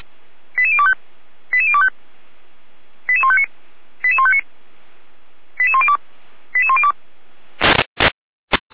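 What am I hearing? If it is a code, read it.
Five-tone selective call (ZVEI), heard as quick runs of five tones, tones of about 80 ms. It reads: 88025 88025 88159 88159 88102 88102
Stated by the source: ZVEI five-tone selective call received via SDR, archived by Feuerwehr Orlamünde.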